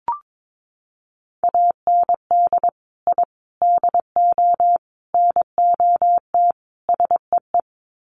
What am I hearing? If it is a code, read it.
Morse code, heard as a keyed tone, about 22 wpm, 700 Hz. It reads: AND I DO NOT SEE
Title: Sets of 5 Words Encore2 22wpm